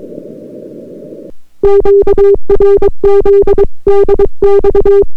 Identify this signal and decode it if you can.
Morse code, heard as a keyed tone, about 22 wpm, 400 Hz. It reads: QRZ DX